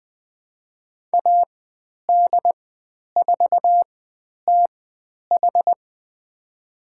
Morse code, heard as a keyed tone, about 20 wpm, spaced as for 10 wpm, 700 Hz. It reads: AD4TH
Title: 109 Lesson 18g - Letter m - Call Signs 20wpm